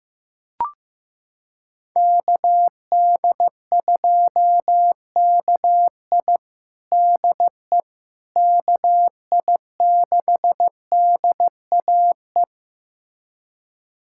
Morse code, heard as a keyed tone, about 15 wpm, 700 Hz. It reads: KD2KI DE KI6DAE
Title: USA Callsign DE Callsign 15wpm